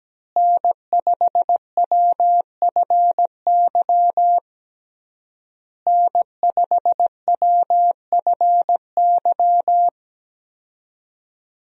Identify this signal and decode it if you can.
Morse code, heard as a keyed tone, about 17 wpm, 700 Hz. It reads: N5WFY N5WFY
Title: USA Call Signs - 5 Characters - Repeated 3x - 17wpm